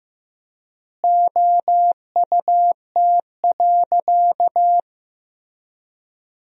Morse code, heard as a keyed tone, about 15 wpm, 700 Hz. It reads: OUT.
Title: 3x Word-Spacing - Sentences from Top 200 Words - 15wpm